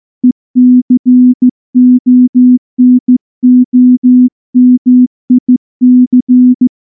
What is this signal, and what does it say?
Morse code, heard as a keyed tone, about 14 wpm, 250 Hz. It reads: ECONOMIC
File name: morse / morse-training-15WPM